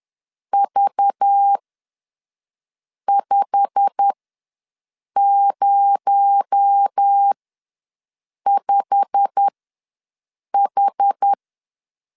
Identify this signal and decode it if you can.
Morse code, heard as a keyed tone, about 10 wpm, spaced as for 5 wpm, 800 Hz. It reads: V505H
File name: LEZIONI / 8 YJ45 / 8 YJ45 ws 10wpm 3843